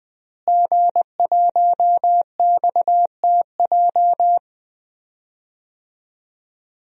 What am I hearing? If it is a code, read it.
Morse code, heard as a keyed tone, about 20 wpm, 700 Hz. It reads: G1XTJ
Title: DXCC-100 Call Signs - 5 Characters - Repeated 3x - 20wpm